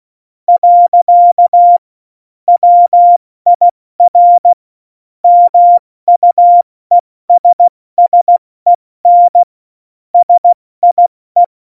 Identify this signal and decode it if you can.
Morse code, heard as a keyed tone, about 16 wpm, spaced as for 14 wpm, 700 Hz. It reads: . WIR MUESSEN SIE
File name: Morsekurs / Witze / Witz 02. Januar 2018, 70 F80